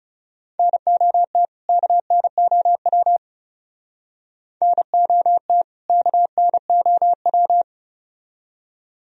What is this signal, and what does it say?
Morse code, heard as a keyed tone, about 37 wpm, 700 Hz. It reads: NOT KNOW NOT KNOW
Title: Speed-Racing - Sets of 2 Words 30wpm